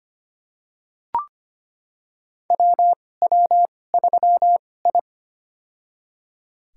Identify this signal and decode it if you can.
Morse code, heard as a keyed tone, about 25 wpm, spaced as for 18 wpm, 700 Hz. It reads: WW3I
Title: USA Call Signs - 2x Character Spacing - 25wpm